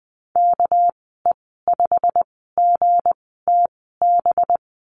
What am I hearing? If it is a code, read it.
Morse code, heard as a keyed tone, about 19 wpm, spaced as for 14 wpm, 700 Hz. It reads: KE5GTB
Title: 231 Lesson 37g - Letter x - Call Signs 20wpm